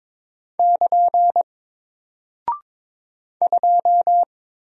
Morse code, heard as a keyed tone, about 22 wpm, 700 Hz. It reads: <KN> 2
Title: Single Character (Letter:Number:Punctuation:Prosign) 22wpm